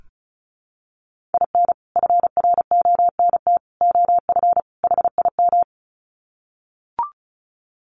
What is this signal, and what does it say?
Morse code, heard as a keyed tone, about 34 wpm, 700 Hz. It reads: IN FRONT OF HIM
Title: Sets of 4 Words 35wpm